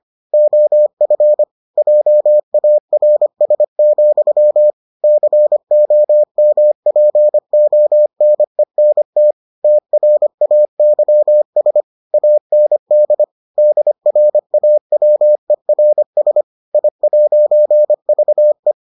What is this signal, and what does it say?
Morse code, heard as a keyed tone, about 25 wpm, 600 Hz. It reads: OF JARS, COMPONENT TRAYS AND DRAWERS I'VE